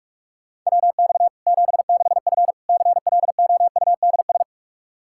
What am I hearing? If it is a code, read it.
Morse code, heard as a keyed tone, about 46 wpm, 700 Hz. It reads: WX 76F CLOUDS
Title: Practice QSO - Exchange 2 - Info - 45wpm